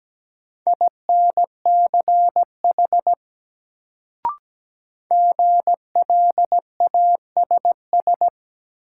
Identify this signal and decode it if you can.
Morse code, heard as a keyed tone, about 17 wpm, 700 Hz. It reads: INCH GLASS
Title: Sentences from Top 1000 Words - Review Part 3 - New Word Drill - 17wpm